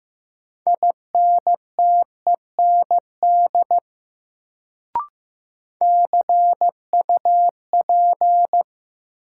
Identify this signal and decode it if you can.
Morse code, heard as a keyed tone, about 15 wpm, 700 Hz. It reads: INTEND CUP
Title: Sentences from Top 1600 Words - New Word Drill - 15wpm